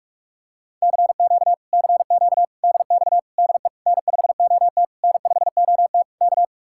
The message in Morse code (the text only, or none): CQ CQ DX DE N5OT N5OT K